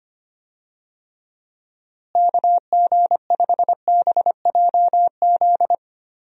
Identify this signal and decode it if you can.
Morse code, heard as a keyed tone, about 25 wpm, 700 Hz. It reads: KG5BJZ
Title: USA Call Signs - 6 Characters - Repeated 2x - 25wpm